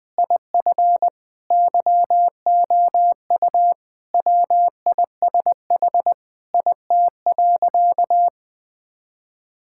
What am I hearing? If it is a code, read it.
Morse code, heard as a keyed tone, about 20 wpm, 700 Hz. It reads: IF YOU WISH IT.
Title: Sentences from Top 1600 Words 20wpm